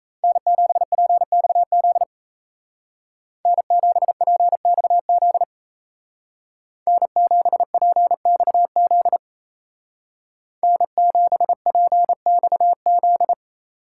Morse code, heard as a keyed tone, about 34 wpm, 700 Hz. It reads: N7PXZ N7PXZ N7PXZ N7PXZ
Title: Speed Racing - USA Call Signs 28wpm